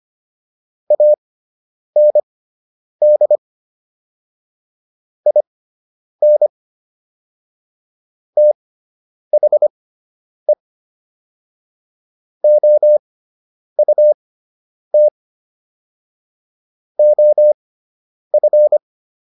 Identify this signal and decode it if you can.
Morse code, heard as a keyed tone, about 25 wpm, spaced as for 9 wpm, 600 Hz. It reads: AND IN THE OUT OF